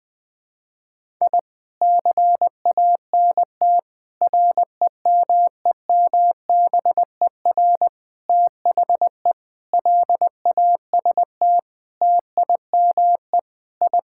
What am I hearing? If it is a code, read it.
Morse code, heard as a keyed tone, about 20 wpm, 700 Hz. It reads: I CANT REMEMBER THE LAST TIME I
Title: Sentences with 7-Word Phrases - 20wpm